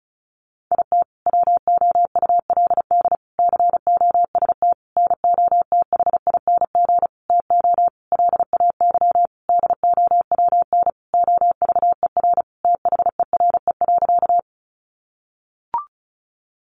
Morse code, heard as a keyed tone, about 34 wpm, 700 Hz. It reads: IT WOULD COST NOTHING TO LAY DOWN OVER THERE.
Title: Sentences from Top 800 Words 35wpm